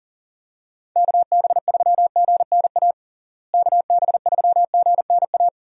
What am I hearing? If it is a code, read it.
Morse code, heard as a keyed tone, about 40 wpm, 700 Hz. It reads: KB3GNA KB3GNA
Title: USA Call Signs - 6 Characters - Repeated 2x - 40wpm